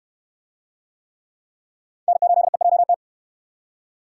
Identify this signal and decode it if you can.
Morse code, heard as a keyed tone, about 62 wpm, 700 Hz. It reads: N8EJT